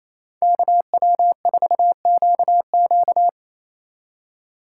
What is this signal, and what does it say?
Morse code, heard as a keyed tone, about 34 wpm, 700 Hz. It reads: KW4QQ
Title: Speed Racing - USA Call Signs 28wpm